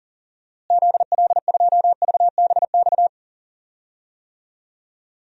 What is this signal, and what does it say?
Morse code, heard as a keyed tone, about 40 wpm, 700 Hz. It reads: ZL2VBX
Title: DXCC-100 Call Signs - 6 Characters - Repeated 2x - 40wpm